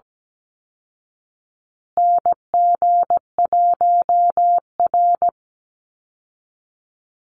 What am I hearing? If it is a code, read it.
Morse code, heard as a keyed tone, about 17 wpm, 700 Hz. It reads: NG1R
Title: Call Signs Repeated 2x - 17wpm